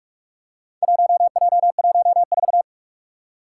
Morse code, heard as a keyed tone, about 44 wpm, 700 Hz. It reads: 1J14
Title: Four Challenging Letter-Number Combinations 45wpm